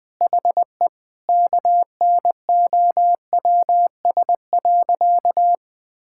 Morse code, heard as a keyed tone, about 20 wpm, 700 Hz. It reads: HE KNOWS.
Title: Sentences with 4-Word Phrases - 20wpm